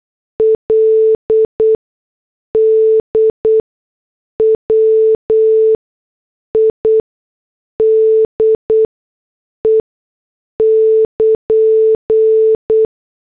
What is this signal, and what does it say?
Morse code, heard as a keyed tone, about 8 wpm, spaced as for 6 wpm, 440 Hz. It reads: LDWIDE <KN>